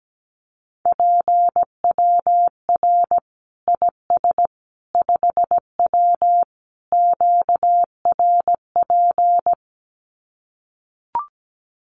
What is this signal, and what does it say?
Morse code, heard as a keyed tone, about 17 wpm, 700 Hz. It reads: PWR IS 5W QRP
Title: Common QSO Phrases 17wpm